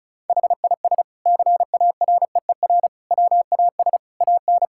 Morse code, heard as a keyed tone, about 34 wpm, 690 Hz. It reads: HIS CAREER WAS AN
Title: Sentences from Top 1000 Words - Review Part 6 - 35wpm